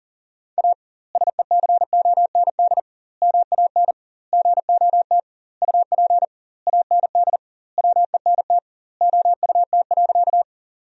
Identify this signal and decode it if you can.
Morse code, heard as a keyed tone, about 40 wpm, 700 Hz. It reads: A SECOND MAN GOT UP AND WENT OUT.
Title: 2x Word-Spacing - Sentences from Top 300 Words - 40wpm